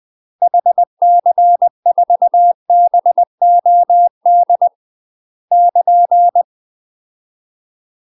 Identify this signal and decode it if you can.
Morse code, heard as a keyed tone, about 20 wpm, 700 Hz. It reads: HC4BOD <KN>
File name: UK MASTER QSOs TRAINING / 20WPM / UK MASTER QSOs 0050